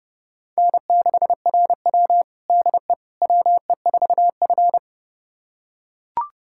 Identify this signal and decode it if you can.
Morse code, heard as a keyed tone, about 30 wpm, 700 Hz. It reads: N6RW DE WE4F
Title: USA Callsign DE Callsign - 4 Characters - 30wpm